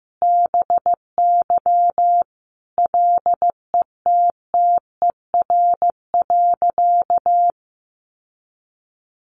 Morse code, heard as a keyed tone, about 15 wpm, 700 Hz. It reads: BY LETTER.
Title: Sentences from Top 1400 Words 15wpm